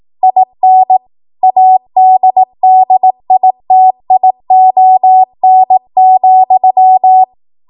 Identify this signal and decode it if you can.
Morse code, heard as a keyed tone, about 18 wpm, 750 Hz. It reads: IN ADDITION,